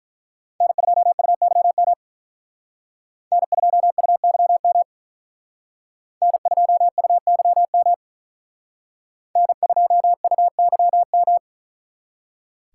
Morse code, heard as a keyed tone, about 43 wpm, 700 Hz. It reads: N2UYM N2UYM N2UYM N2UYM
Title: Speed Racing - USA 1x3 Call Signs 35wpm